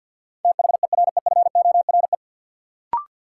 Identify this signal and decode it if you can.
Morse code, heard as a keyed tone, about 50 wpm, 700 Hz. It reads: THEREFORE
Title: Sentences from Top 800 Words - New Word Drill - 50wpm